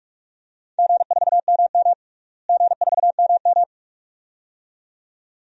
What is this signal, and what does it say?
Morse code, heard as a keyed tone, about 46 wpm, 700 Hz. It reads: G4MM G4MM
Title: DXCC-100 Call Signs - 4 Characters - Repeated 2x - 45wpm